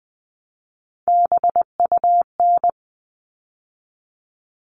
Morse code, heard as a keyed tone, about 20 wpm, 700 Hz. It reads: BUN
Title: All Cut Numbers - 3-Digit 20wpm